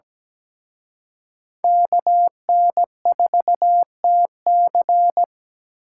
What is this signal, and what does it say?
Morse code, heard as a keyed tone, about 20 wpm, 700 Hz. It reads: KN4TC
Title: Speed Racing - USA 2x2 Call Signs 17wpm